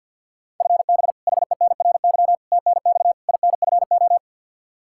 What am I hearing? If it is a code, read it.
Morse code, heard as a keyed tone, about 49 wpm, 700 Hz. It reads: FB HENRY TNX INFO
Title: Practice QSO - Exchange 3 - More Info - 50wpm